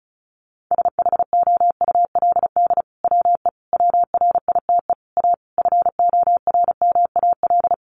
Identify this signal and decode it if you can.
Morse code, heard as a keyed tone, about 34 wpm, 700 Hz. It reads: SHOULD WE WRITE A FORMAL
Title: Sentences from Top 1300 Words 35wpm